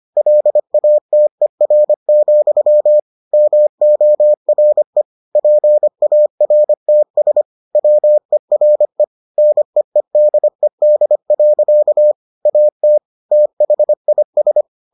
Morse code, heard as a keyed tone, about 25 wpm, 600 Hz. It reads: LATER, MORE PARTS WERE NEEDED. AT THIS